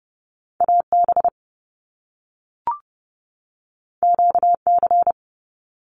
Morse code, heard as a keyed tone, about 30 wpm, 700 Hz. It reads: AB QC
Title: Canadian Provinces and Territories 30wpm